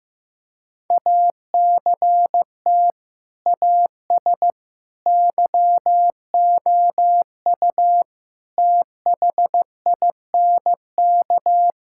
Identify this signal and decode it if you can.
Morse code, heard as a keyed tone, about 15 wpm, 700 Hz. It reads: ACT AS YOU THINK
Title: Sentences from Top 1000 Words - Review Part 5 - 15wpm